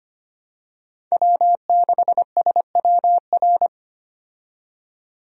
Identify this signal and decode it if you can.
Morse code, heard as a keyed tone, about 25 wpm, 700 Hz. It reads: W6SWR
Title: USA Call Signs - 5 Characters - 25wpm